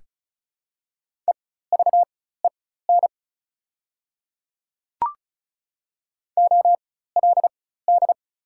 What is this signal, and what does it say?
Morse code, heard as a keyed tone, about 34 wpm, spaced as for 16 wpm, 700 Hz. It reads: EVEN OLD
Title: 4x Character Spacing - Top 300 Words - 35wpm